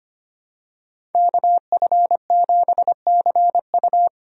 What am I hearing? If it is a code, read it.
Morse code, heard as a keyed tone, about 25 wpm, 700 Hz. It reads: KF7CU
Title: USA Call Signs - 5 Characters - 25wpm